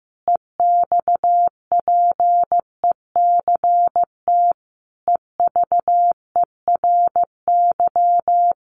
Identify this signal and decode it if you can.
Morse code, heard as a keyed tone, about 15 wpm, 700 Hz. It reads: EXPECT EVERY
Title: Sentences from Top 1100 Words 15wpm